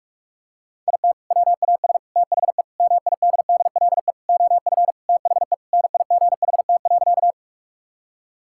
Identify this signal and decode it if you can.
Morse code, heard as a keyed tone, about 46 wpm, 690 Hz. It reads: IT WAS THE MIDDLE OF THE NIGHT.
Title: Sentences from Top 1000 Words - Review Part 6 - 45wpm